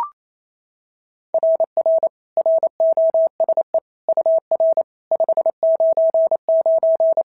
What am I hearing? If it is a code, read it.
Morse code, heard as a keyed tone, about 27 wpm, 690 Hz. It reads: RR ROSE UR 599
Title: SOTA QSO Practice 28wpm